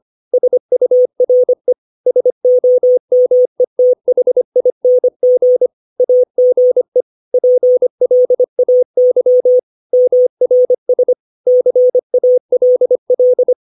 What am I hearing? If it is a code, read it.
Morse code, heard as a keyed tone, about 25 wpm, 500 Hz. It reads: SURE SOMETHING AGE PLAY MRS CALL
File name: 632 words / 632 Words 25WPM